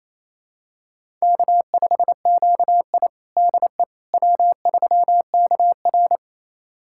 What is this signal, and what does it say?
Morse code, heard as a keyed tone, about 28 wpm, 700 Hz. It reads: K5QI DE W3KR
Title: USA Callsign DE Callsign - 4 Characters - 28wpm